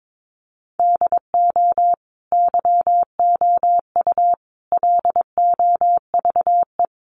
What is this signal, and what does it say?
Morse code, heard as a keyed tone, about 22 wpm, 700 Hz. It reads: DO YOU LOVE
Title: Sentences from Top 400 Words 22wpm